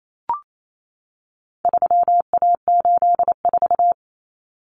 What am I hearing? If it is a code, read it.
Morse code, heard as a keyed tone, about 28 wpm, 700 Hz. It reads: 3A84
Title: Common Cut Numbers - 4-Digit 28wpm